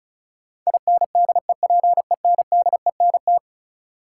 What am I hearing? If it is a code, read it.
Morse code, heard as a keyed tone, about 34 wpm, 700 Hz. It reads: INDEPENDENT
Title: Sentences from Top 900 Words - New Word Drill - 35wpm